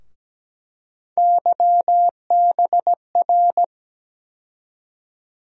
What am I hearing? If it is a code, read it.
Morse code, heard as a keyed tone, about 17 wpm, 700 Hz. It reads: YBR
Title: USA Call Sign Postfixes - 3 Letters 17wpm